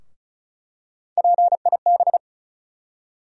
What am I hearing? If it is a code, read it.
Morse code, heard as a keyed tone, about 36 wpm, 700 Hz. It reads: PIB
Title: USA Call Sign Postfixes - 3 Letters 35wpm